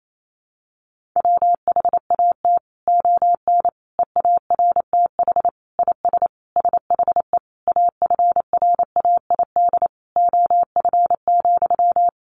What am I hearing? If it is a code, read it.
Morse code, heard as a keyed tone, about 28 wpm, 700 Hz. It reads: WHAT ON EARTH IS SHE AFRAID OF,
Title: Sentences from Top 1500 Words 28wpm